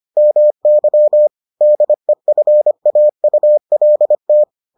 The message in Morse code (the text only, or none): MY DEFAULT